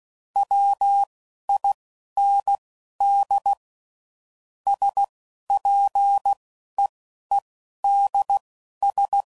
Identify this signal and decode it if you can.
Morse code, heard as a keyed tone, about 16 wpm, spaced as for 11 wpm, 800 Hz. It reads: WIND SPEEDS
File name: CW Code Groups @10WPM / Weather Sets / Weather 4 21